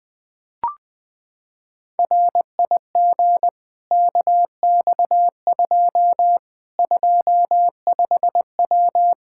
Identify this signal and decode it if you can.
Morse code, heard as a keyed tone, about 21 wpm, 700 Hz. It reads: RIG KX2 25W
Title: K1USN SST QSO Practice 20wpm